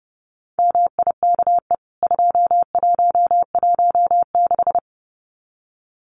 Morse code, heard as a keyed tone, about 30 wpm, 700 Hz. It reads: MIKE 2116
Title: CWT Exchange 30wpm